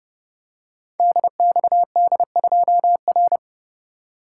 Morse code, heard as a keyed tone, about 30 wpm, 700 Hz. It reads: DXD2R
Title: Code Groups - 5 Letter:Numbers - 30wpm